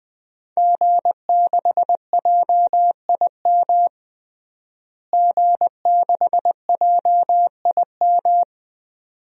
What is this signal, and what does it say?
Morse code, heard as a keyed tone, about 20 wpm, 700 Hz. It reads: G6JIM G6JIM
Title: DXCC-100 Call Signs - 5 Characters - Repeated 2x - 20wpm